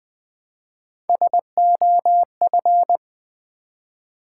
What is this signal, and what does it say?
Morse code, heard as a keyed tone, about 20 wpm, 700 Hz. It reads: SOF
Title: Build A Word - 1 Letter At A Time - 4-Letter Words - 20wpm